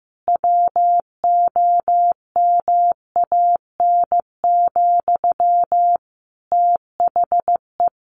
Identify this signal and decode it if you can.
Morse code, heard as a keyed tone, about 15 wpm, 700 Hz. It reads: WOMAN, THE
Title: Sentences from Top 1500 Words 15wpm